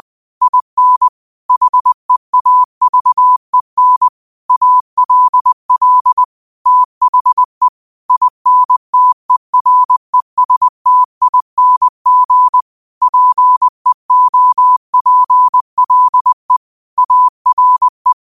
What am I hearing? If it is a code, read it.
Morse code, heard as a keyed tone, about 20 wpm, 1000 Hz. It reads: IN HEAVEN ALL THE INTERESTING PEOPLE ARE